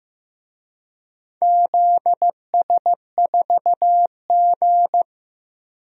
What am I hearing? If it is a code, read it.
Morse code, heard as a keyed tone, about 15 wpm, 700 Hz. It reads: ZS4G
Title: DXCC-100 Call Signs - 4 Characters - Repeated 2x - 15wpm